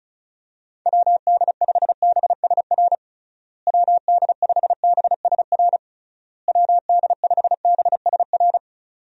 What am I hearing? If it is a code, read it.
Morse code, heard as a keyed tone, about 34 wpm, 700 Hz. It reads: WD5BSR WD5BSR WD5BSR
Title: USA Call Signs - 6 Characters - Repeated 3x - 35wpm